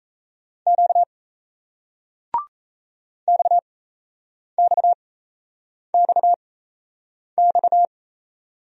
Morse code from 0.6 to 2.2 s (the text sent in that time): Q